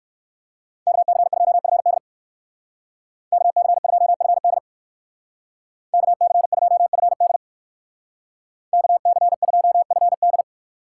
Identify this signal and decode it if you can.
Morse code, heard as a keyed tone, about 57 wpm, 700 Hz. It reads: KC2FD KC2FD KC2FD KC2FD